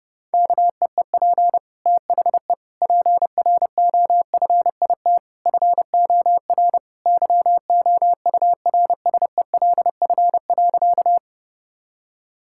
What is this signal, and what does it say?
Morse code, heard as a keyed tone, about 30 wpm, 700 Hz. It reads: KEEP THE PROFIT FOR YOURSELF.
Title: Sentences from Top 1000 Words - Review Part 5 - 30wpm